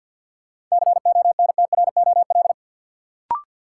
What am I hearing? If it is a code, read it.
Morse code, heard as a keyed tone, about 48 wpm, 700 Hz. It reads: CONTROL
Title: Sentences from Top 1200 Words - New Word Drill Alt - 50wpm